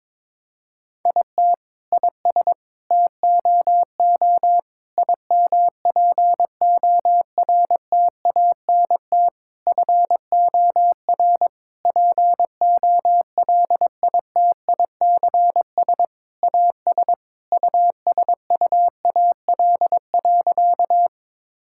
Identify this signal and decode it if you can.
Morse code, heard as a keyed tone, about 21 wpm, 700 Hz. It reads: IT IS TOO IMPORTANT FOR POLITICS AS USUAL.
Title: Sentences from Top 1000 Words - Review Part 6 - 22wpm